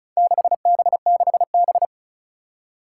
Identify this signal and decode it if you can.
Morse code, heard as a keyed tone, about 36 wpm, 700 Hz. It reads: 6B6B
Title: Four Challenging Letter-Number Combinations 35wpm